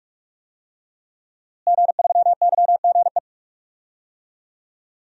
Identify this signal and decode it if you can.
Morse code, heard as a keyed tone, about 46 wpm, 700 Hz. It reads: G3YGE
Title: DXCC-100 Call Signs - 5 Characters - Repeated 2x - 45wpm